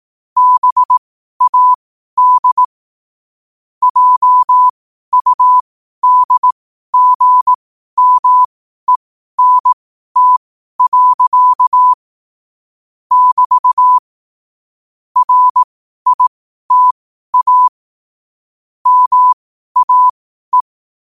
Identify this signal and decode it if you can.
Morse code, heard as a keyed tone, about 18 wpm, spaced as for 12 wpm, 1000 Hz. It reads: BAD JUDGMENT. <BT> RITA MAE
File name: qotdfiles / 12wpm